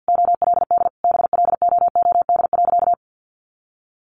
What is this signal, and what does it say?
Morse code, heard as a keyed tone, about 49 wpm, 700 Hz. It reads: OLD BLOOD.
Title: A Sound Of Thunder 50wpm